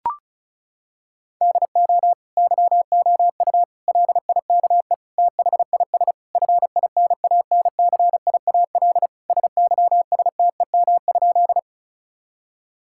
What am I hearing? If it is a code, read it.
Morse code, heard as a keyed tone, about 34 wpm, 700 Hz. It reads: DO YOU LIKE THIS FINANCIAL SYSTEM?
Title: Sentences from Top 1000 Words - Review Part 5 - 35wpm